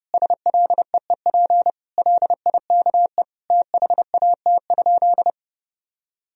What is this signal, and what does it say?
Morse code, heard as a keyed tone, about 30 wpm, 700 Hz. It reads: SLEEP LIKE THAT?
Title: Sentences from Top 1000 Words - Review Part 4 - 30wpm